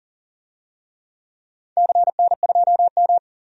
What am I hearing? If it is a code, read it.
Morse code, heard as a keyed tone, about 38 wpm, 700 Hz. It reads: CN2M
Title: DXCC-100 Call Signs - 4 Characters - 40wpm